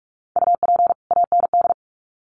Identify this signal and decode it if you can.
Morse code, heard as a keyed tone, about 44 wpm, 700 Hz. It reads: UP AND
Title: Sets of 2 Words 45wpm